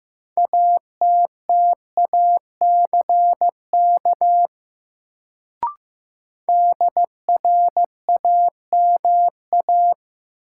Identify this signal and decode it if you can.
Morse code, heard as a keyed tone, about 15 wpm, 700 Hz. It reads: ATTACK DRAMA
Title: Sentences from Top 1500 Words - New Word Drill Alt - 15wpm